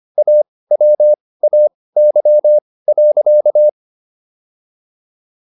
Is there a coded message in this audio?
Yes: Morse code from 0.2 to 3.7 s, heard as a keyed tone, about 25 wpm, spaced as for 18 wpm, 600 Hz. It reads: AWAY.